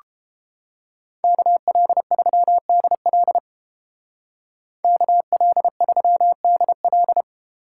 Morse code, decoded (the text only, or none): KL3DL KL3DL